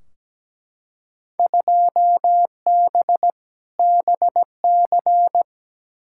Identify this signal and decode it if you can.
Morse code, heard as a keyed tone, about 17 wpm, 700 Hz. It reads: 2B BC
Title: Field Day Exchanges Encore - 17wpm